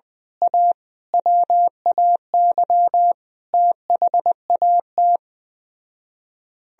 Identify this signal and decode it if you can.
Morse code, heard as a keyed tone, about 17 wpm, 700 Hz. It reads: A WAY THAT
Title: Speed-Racing - Sets of 5 Words 17wpm